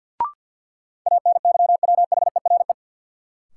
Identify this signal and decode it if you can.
Morse code, heard as a keyed tone, about 53 wpm, 700 Hz. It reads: ANYWHERE
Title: Sentences from Top 1400 Words - New Word Drill - 50wpm